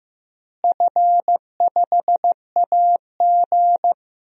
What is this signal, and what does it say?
Morse code, heard as a keyed tone, about 15 wpm, 700 Hz. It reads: F5AG
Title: DXCC-100 Call Signs - 4 Characters - Repeated 3x - 15wpm